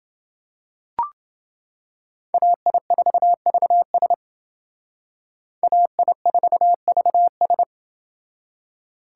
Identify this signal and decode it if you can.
Morse code, heard as a keyed tone, about 29 wpm, 700 Hz. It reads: AI4VS AI4VS